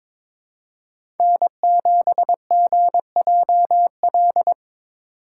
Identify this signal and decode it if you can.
Morse code, heard as a keyed tone, about 22 wpm, 700 Hz. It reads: N7GJL
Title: USA Call Signs - 5 Characters - 2x Character Spacing - 22wpm